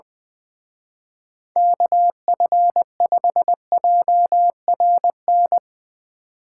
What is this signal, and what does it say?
Morse code, heard as a keyed tone, about 21 wpm, 700 Hz. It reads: KF5JRN
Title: Speed Racing - USA 2x3 Call Signs 20wpm